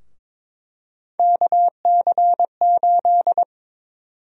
Morse code, heard as a keyed tone, about 22 wpm, 700 Hz. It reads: KC8
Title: USA Call Sign Prefixes - 22wpm